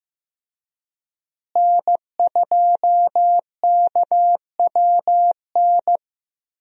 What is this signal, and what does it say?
Morse code, heard as a keyed tone, about 16 wpm, 700 Hz. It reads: N2KWN
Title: USA Call Signs - 5 Characters - Repeated 3x - 15wpm